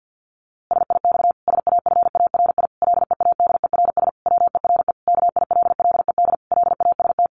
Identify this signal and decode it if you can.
Morse code, heard as a keyed tone, about 50 wpm, 700 Hz. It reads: SIX SAFARI LEADERS WERE KILLED LAST
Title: A Sound Of Thunder 50wpm